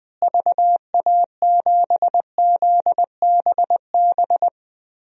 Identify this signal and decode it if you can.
Morse code, heard as a keyed tone, about 20 wpm, 700 Hz. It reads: VA7ZBB
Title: DXCC-100 Call Signs - 6 Characters - Repeated 2x - 20wpm